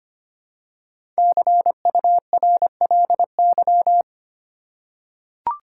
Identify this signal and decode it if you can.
Morse code, heard as a keyed tone, about 25 wpm, 700 Hz. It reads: CURLY
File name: 5 Letter Words - First 3 Letters Given - 25wpm